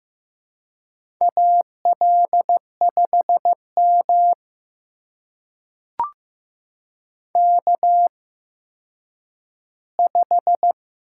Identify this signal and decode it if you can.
Morse code, heard as a keyed tone, about 15 wpm, spaced as for 9 wpm, 700 Hz. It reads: AL5M K5
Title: USA Call Signs - 8x Character Spacing - 15wpm